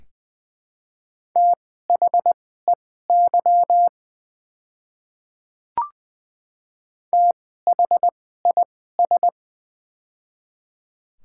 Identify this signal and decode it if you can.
Morse code, heard as a keyed tone, about 21 wpm, spaced as for 15 wpm, 700 Hz. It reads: THEY THIS
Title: 2x Character Spacing - Top 100 Words - 20wpm